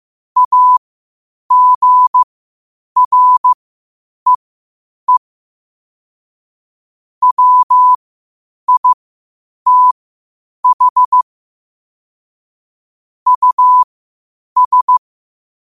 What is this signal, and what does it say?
Morse code, heard as a keyed tone, about 15 wpm, spaced as for 8 wpm, 1000 Hz. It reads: AGREE WITH US
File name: qotdfiles / 8wpm